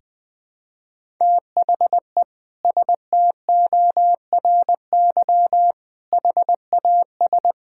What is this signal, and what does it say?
Morse code, heard as a keyed tone, about 20 wpm, 700 Hz. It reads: THE STORY HAS
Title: Sentences with 7-Word Phrases - Spoken Beginning - 20wpm